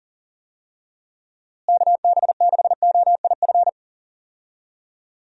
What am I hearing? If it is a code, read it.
Morse code, heard as a keyed tone, about 40 wpm, 700 Hz. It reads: KB6OIF
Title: USA Call Signs - 6 Characters - Repeated 2x - 40wpm